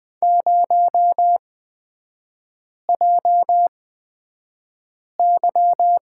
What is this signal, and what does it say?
Morse code, heard as a keyed tone, about 20 wpm, spaced as for 5 wpm, 700 Hz. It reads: 0JY
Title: 243 Lesson 39e - Letter z - Three Characters 20wpm